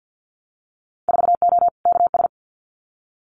Morse code, heard as a keyed tone, about 50 wpm, 700 Hz. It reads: 4O KS
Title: Winter Field Day Exchanges Encore 50wpm